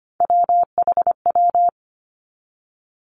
Morse code, heard as a keyed tone, about 25 wpm, 700 Hz. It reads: WHW
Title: Three Challenging Letter-Number Combinations 25wpm